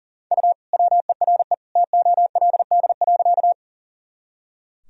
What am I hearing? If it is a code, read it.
Morse code, heard as a keyed tone, about 40 wpm, 700 Hz. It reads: U WERE TOLD.
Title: Sentences from Top 400 Words and Abbreviations 40wpm